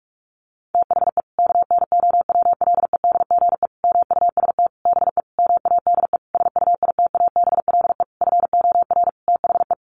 Morse code, heard as a keyed tone, about 44 wpm, 700 Hz. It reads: THE KNOWLEDGE MUST BE MADE SUITABLE FOR THE
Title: Sentences from Top 1400 Words 45wpm